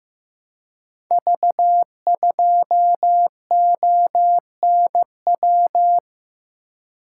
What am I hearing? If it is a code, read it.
Morse code, heard as a keyed tone, about 15 wpm, 700 Hz. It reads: V2ONW
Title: Code Groups - 5 Letter:Numbers - 15wpm